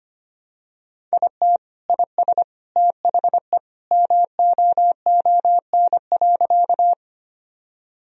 Morse code, heard as a keyed tone, about 25 wpm, 700 Hz. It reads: IT IS THE MOON.
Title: Sentences from Top 1000 Words - Review Part 3 - 25wpm